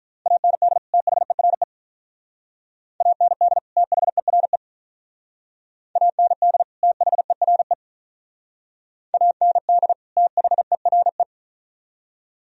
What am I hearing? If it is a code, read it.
Morse code, heard as a keyed tone, about 43 wpm, 700 Hz. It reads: AND THERE AND THERE AND THERE AND THERE
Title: Speed-Racing - Sets of 2 Words 35wpm